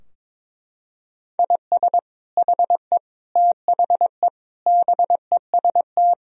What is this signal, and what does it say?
Morse code, heard as a keyed tone, about 22 wpm, 700 Hz. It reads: IS HE THE BEST
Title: Sentences from Top 1000 Words - Review Part 4 - 22wpm